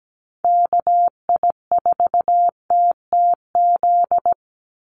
Morse code, heard as a keyed tone, about 17 wpm, 700 Hz. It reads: KI4TTZ
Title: USA Call Signs 17wpm